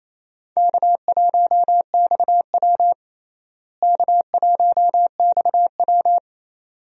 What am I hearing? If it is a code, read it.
Morse code, heard as a keyed tone, about 28 wpm, 700 Hz. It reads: K1XW K1XW
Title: USA Call Signs - 4 Characters - Repeated 2x - 28wpm